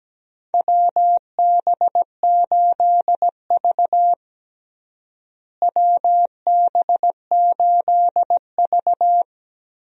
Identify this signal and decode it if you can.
Morse code, heard as a keyed tone, about 17 wpm, 700 Hz. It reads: WB8V WB8V
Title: USA Call Signs - 4 Characters - Repeated 3x - 17wpm